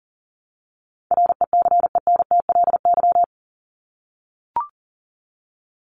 Morse code, heard as a keyed tone, about 40 wpm, 700 Hz. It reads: RECENTLY
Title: LY Suffix - Top 100 Words with Base Words 40wpm